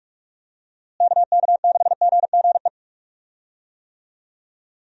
Morse code, heard as a keyed tone, about 43 wpm, 700 Hz. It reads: KK6GGE
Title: USA Call Signs - 6 Characters - Repeated 2x - 45wpm